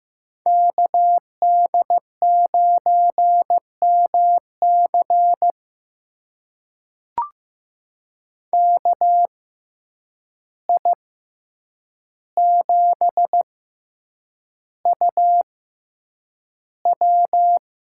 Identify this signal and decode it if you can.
Morse code, heard as a keyed tone, about 15 wpm, spaced as for 8 wpm, 700 Hz. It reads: KD9MC KI7UW
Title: USA Call Signs - 5 Characters - 6x Character Spacing - 15wpm